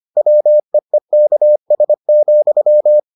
Morse code, heard as a keyed tone, about 25 wpm, 600 Hz. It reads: WEEKS,